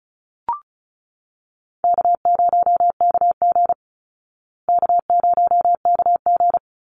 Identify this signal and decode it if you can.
Morse code, heard as a keyed tone, about 34 wpm, 700 Hz. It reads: K0KG K0KG
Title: Easy Call Signs Repeated 2x - 35wpm